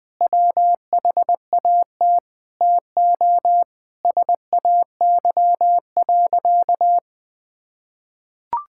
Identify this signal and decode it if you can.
Morse code, heard as a keyed tone, about 20 wpm, 700 Hz. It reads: WHAT TO SAY.
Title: Sentences with 3-Word Phrases - 20wpm